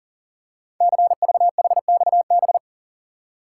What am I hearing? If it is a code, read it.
Morse code, heard as a keyed tone, about 40 wpm, 700 Hz. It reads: CVHXB
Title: Code Groups - 5 Letters - 40wpm